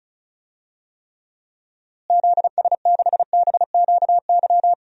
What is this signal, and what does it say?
Morse code, heard as a keyed tone, about 36 wpm, 700 Hz. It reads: ZS6BQY
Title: DXCC-100 Call Signs - 6 Characters - Repeated 3x - 35wpm